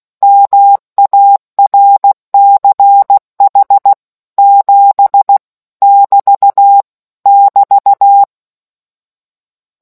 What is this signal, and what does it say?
Morse code, heard as a keyed tone, about 16 wpm, 800 Hz. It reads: MARCH 7 <BT> <BT>